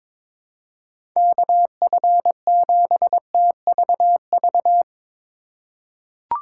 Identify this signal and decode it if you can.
Morse code, heard as a keyed tone, about 22 wpm, 700 Hz. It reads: KF7TVV
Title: USA Call Signs - 6 Characters - 3x Character Spacing - 22wpm